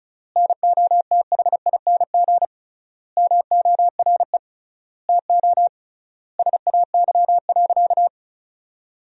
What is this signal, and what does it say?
Morse code, heard as a keyed tone, about 36 wpm, 700 Hz. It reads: NOTHING MORE TO SAY.
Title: 3x Word-Spacing - Sentences from Top 500 Words - 35wpm